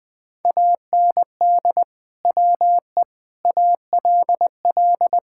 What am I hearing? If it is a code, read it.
Morse code, heard as a keyed tone, about 20 wpm, 700 Hz. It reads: AND WE ALL
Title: Sentences with 2-Word Phrases - Spoken Beginning - 20wpm